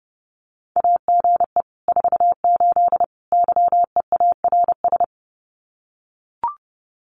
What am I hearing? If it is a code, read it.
Morse code, heard as a keyed tone, about 30 wpm, 700 Hz. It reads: AGE 48 YEARS
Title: Common QSO Phrases Encore 30wpm